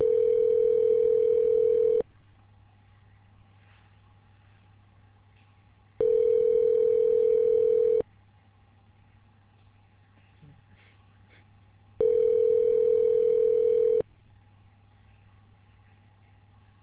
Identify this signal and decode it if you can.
Ringback tone (North America), 440 + 480 Hz.